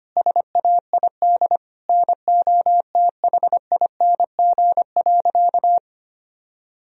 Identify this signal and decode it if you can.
Morse code, heard as a keyed tone, about 25 wpm, 700 Hz. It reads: SAID NOTHING.
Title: Sentences from Top 1000 Words - Review Part 2 - 25wpm